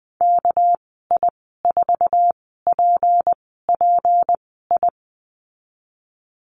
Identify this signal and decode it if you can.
Morse code, heard as a keyed tone, about 20 wpm, spaced as for 14 wpm, 700 Hz. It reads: KI4PPI
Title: 238 Lesson 38g - Letter q - Call Signs 20wpm